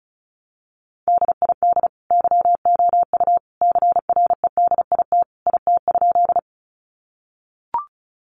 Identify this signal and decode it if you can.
Morse code, heard as a keyed tone, about 34 wpm, 700 Hz. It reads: DID YOU CREDIT IT?
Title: Sentences from Top 1000 Words 35wpm